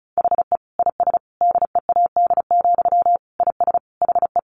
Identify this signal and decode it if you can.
Morse code, heard as a keyed tone, about 34 wpm, 700 Hz. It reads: HE IS DEAD, IS HE
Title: Sentences from Top 1500 Words 35wpm